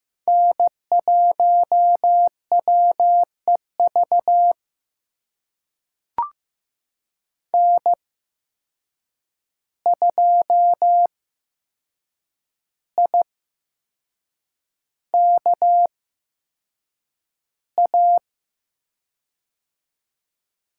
Morse code, heard as a keyed tone, about 16 wpm, spaced as for 6 wpm, 700 Hz. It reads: N1WEV N2IKA
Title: USA Call Signs - 5 Characters - 8x Character Spacing - 15wpm